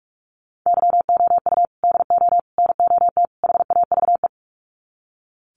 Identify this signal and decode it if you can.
Morse code, heard as a keyed tone, about 44 wpm, 700 Hz. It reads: YOU DO NOT HAVE
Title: Sets of 4 Words 45wpm